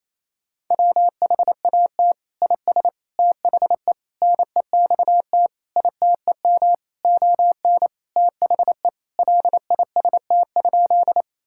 Morse code, heard as a keyed tone, about 27 wpm, 700 Hz. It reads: WHAT IS THE NEXT ITEM ON THE LIST?
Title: Sentences from Top 1000 Words - Review Part 6 - 28wpm